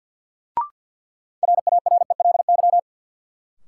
Morse code, heard as a keyed tone, about 53 wpm, 700 Hz. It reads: RARELY
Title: Sentences from Top 1400 Words - New Word Drill - 50wpm